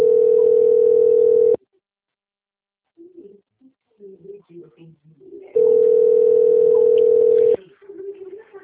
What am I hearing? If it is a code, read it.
Ringback tone (North America), 440 + 480 Hz, with speech.
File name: rg-1600-unknown- - - .37259